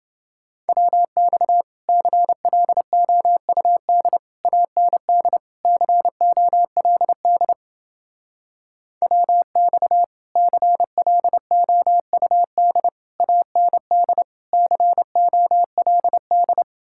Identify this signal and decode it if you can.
Morse code, heard as a keyed tone, about 29 wpm, 700 Hz. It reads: WX CLOUD AND COLD WX CLOUD AND COLD